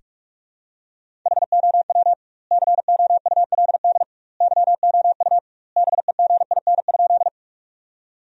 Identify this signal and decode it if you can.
Morse code, heard as a keyed tone, about 41 wpm, 700 Hz. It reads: HOW COULD YOU BEGIN?
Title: 2x Word-Spacing - Sentences from Top 300 Words - 45wpm